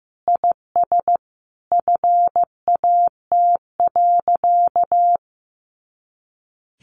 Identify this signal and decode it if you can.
Morse code, heard as a keyed tone, about 15 wpm, 700 Hz. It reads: IS FAT.
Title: Easy Phrases 15wpm